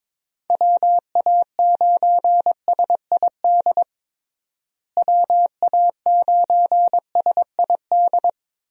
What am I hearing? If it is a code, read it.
Morse code, heard as a keyed tone, about 22 wpm, 700 Hz. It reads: WA9SID WA9SID
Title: USA Call Signs - 6 Characters - Repeated 2x - 22wpm